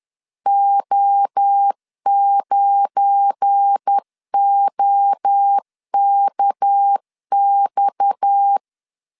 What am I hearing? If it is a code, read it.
Morse code, heard as a keyed tone, about 10 wpm, 800 Hz. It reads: O9OKX
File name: LEZIONI / 8 YJ45 / 8 YJ4 10wpm 2633